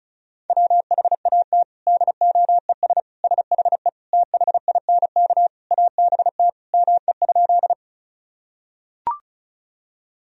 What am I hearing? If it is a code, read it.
Morse code, heard as a keyed tone, about 35 wpm, 700 Hz. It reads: WHAT DOES SHE THINK ABT ME?
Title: Sentences from Top 200 Words and Abbreviations 35wpm